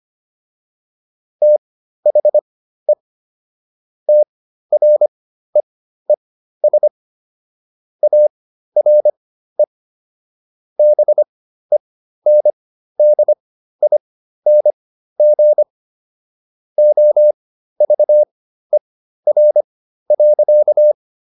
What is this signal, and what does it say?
Morse code, heard as a keyed tone, about 25 wpm, spaced as for 13 wpm, 600 Hz. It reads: THE TREES ARE BENDING OVER.